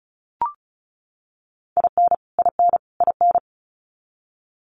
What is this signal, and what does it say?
Morse code, heard as a keyed tone, about 34 wpm, 700 Hz. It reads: IN IN IN
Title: Top 500 Words Repeated 35wpm